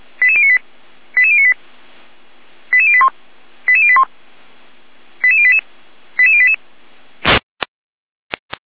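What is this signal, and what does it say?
Five-tone selective call (ZVEI), heard as quick runs of five tones, tones of about 70 ms. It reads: 88098 88098 88081 88081 88088 88088
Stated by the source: ZVEI five-tone selective call received on an SDR.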